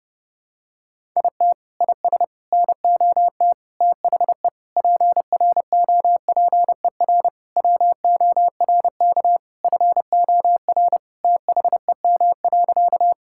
Morse code, heard as a keyed tone, about 32 wpm, 700 Hz. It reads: IT IS NOT THE PROPER WORK FOR THEM.
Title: Sentences from Top 1000 Words - Review Part 5 - 30wpm